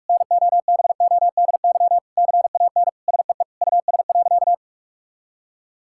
Morse code, heard as a keyed tone, about 46 wpm, 700 Hz. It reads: NOBODY CAN SEE US.
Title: Sentences from Top 1600 Words 45wpm